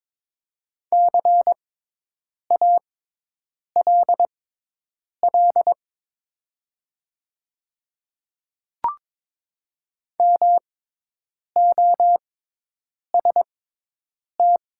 Morse code, heard as a keyed tone, about 22 wpm, spaced as for 8 wpm, 700 Hz. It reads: CALL MOST
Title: 6x Character Spacing - Top 100 Words - 22wpm